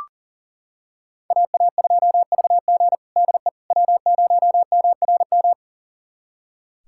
Morse code, heard as a keyed tone, about 40 wpm, 700 Hz. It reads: AA2VG DE W0MRM
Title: USA Callsign DE Callsign - 5 Characters - 40wpm